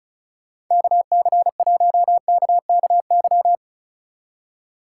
Morse code, heard as a keyed tone, about 44 wpm, 700 Hz. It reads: KC1KKY